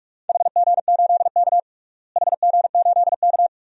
Morse code, heard as a keyed tone, about 47 wpm, 700 Hz. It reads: HG8K HG8K
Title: DXCC-100 Call Signs - 4 Characters - Repeated 3x - 45wpm